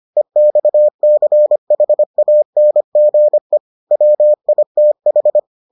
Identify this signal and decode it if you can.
Morse code, heard as a keyed tone, about 25 wpm, 600 Hz. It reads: EXCHANGE WITH